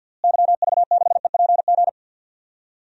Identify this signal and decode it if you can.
Morse code, heard as a keyed tone, about 50 wpm, 700 Hz. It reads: YV6EPG
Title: DXCC-100 Call Signs - 6 Characters - Repeated 2x - 50wpm